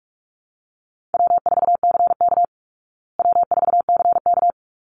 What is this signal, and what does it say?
Morse code, heard as a keyed tone, about 44 wpm, 700 Hz. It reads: W4CK W4CK
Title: Easy Call Signs Repeated 2x - 45wpm